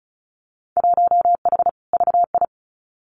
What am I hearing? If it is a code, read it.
Morse code, heard as a keyed tone, about 34 wpm, 700 Hz. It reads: 1H VI
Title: Winter Field Day Exchanges Encore 35wpm